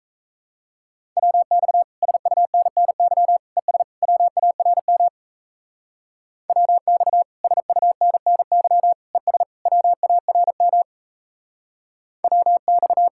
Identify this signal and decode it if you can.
Morse code, heard as a keyed tone, about 33 wpm, 700 Hz. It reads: WX SUNNY ES WARM WX SUNNY ES WARM WX
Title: Speed-Racing Common QSO Phrases Encore - 28wpm